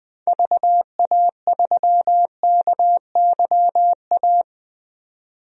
Morse code, heard as a keyed tone, about 20 wpm, 700 Hz. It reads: VA3KYA
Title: DXCC-100 Call Signs - 6 Characters - 20wpm